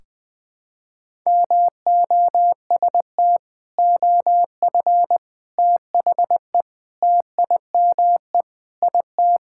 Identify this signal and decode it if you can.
Morse code, heard as a keyed tone, about 21 wpm, 700 Hz. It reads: MOST OF THE TIME IT
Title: Sentences with 3-Word Phrases - 20wpm